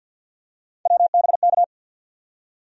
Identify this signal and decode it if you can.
Morse code, heard as a keyed tone, about 51 wpm, 700 Hz. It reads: WBK
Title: USA Call Sign Postfixes - 3 Letters 50wpm